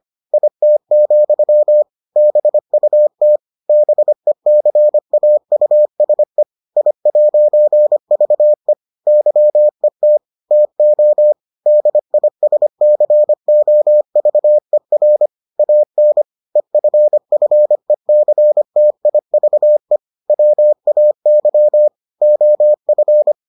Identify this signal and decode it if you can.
Morse code, heard as a keyed tone, about 25 wpm, 600 Hz. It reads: IT, BUT BECAUSE I'VE YET TO DISCOVER AN EFFECTIVE WAY OF